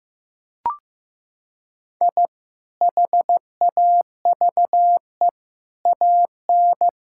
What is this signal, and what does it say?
Morse code, heard as a keyed tone, about 16 wpm, 700 Hz. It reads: I HAVE AN
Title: Sentences from Top 1000 Words - Review Part 5 - 15wpm